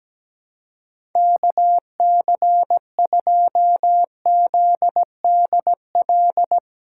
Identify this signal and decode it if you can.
Morse code, heard as a keyed tone, about 17 wpm, 700 Hz. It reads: KC2ZDL
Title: USA Callsign DE Callsign 17wpm